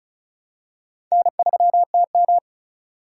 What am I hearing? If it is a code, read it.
Morse code, heard as a keyed tone, about 36 wpm, 700 Hz. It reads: N3TM
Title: USA Call Signs - 4x Character Spacing - 35wpm